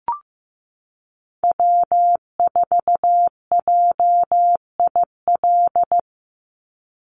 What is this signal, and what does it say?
Morse code, heard as a keyed tone, about 15 wpm, 700 Hz. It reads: W4JIL
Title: USA Call Signs 15wpm